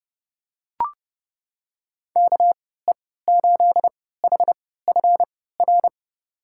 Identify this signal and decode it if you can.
Morse code, heard as a keyed tone, about 30 wpm, spaced as for 17 wpm, 700 Hz. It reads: KE8HFR
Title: USA Call Signs - 6 Characters - 3x Character Spacing - 30wpm